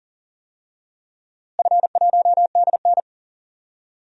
Morse code, heard as a keyed tone, about 40 wpm, 700 Hz. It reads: F1DN
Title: DXCC-100 Call Signs - 4 Characters - Repeated 3x - 40wpm